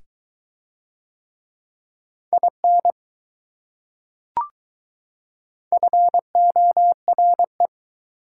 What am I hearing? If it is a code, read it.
Morse code, heard as a keyed tone, about 20 wpm, 700 Hz. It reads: IN FORE